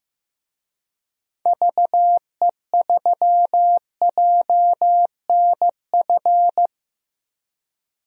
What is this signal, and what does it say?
Morse code, heard as a keyed tone, about 15 wpm, 700 Hz. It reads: VE3JNF
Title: DXCC-100 Call Signs - 6 Characters - 15wpm